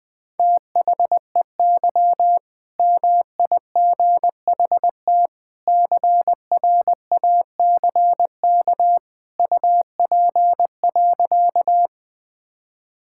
Morse code, heard as a keyed tone, about 21 wpm, 700 Hz. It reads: THEY MIGHT CRACK UP.
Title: All-Age Appropriate Jokes - 20wpm